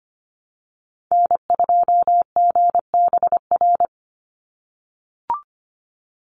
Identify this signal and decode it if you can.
Morse code, heard as a keyed tone, about 25 wpm, 700 Hz. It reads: N2GBR
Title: NA SOTA Activator Callsigns 25wpm